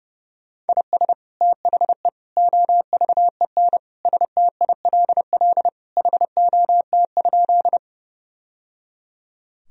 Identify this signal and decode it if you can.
Morse code, heard as a keyed tone, about 30 wpm, 700 Hz. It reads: IS THE OVEN STILL HOT?
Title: Sentences from Top 1000 Words - Review Part 4 - 30wpm